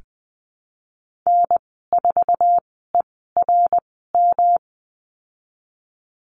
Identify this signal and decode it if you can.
Morse code, heard as a keyed tone, about 20 wpm, spaced as for 14 wpm, 700 Hz. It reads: N4ERM
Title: 154 Lesson 25g - Letter p - Call Signs 20wpm